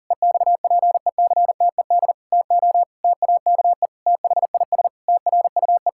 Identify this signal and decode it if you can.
Morse code, heard as a keyed tone, about 40 wpm, 700 Hz. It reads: EXPECTED TO TAKE THIS TRUE
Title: Pangram Challenge v2 - 40wpm